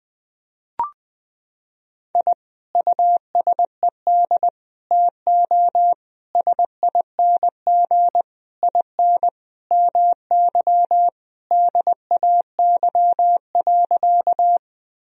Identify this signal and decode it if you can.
Morse code, heard as a keyed tone, about 20 wpm, 700 Hz. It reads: I USED TO SING IN MY DAY.
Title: Sentences from Top 1000 Words - Review Part 2 - 20wpm